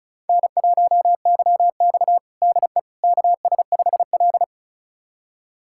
Morse code, heard as a keyed tone, about 36 wpm, 700 Hz. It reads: N1YX DE KS5L
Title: USA Callsign DE Callsign - 4 Characters - 35wpm